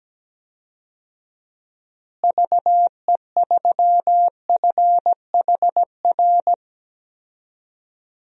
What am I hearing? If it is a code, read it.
Morse code, heard as a keyed tone, about 17 wpm, 700 Hz. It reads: VE3FHR
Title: DXCC-100 Call Signs - 6 Characters - Repeated 2x - 17wpm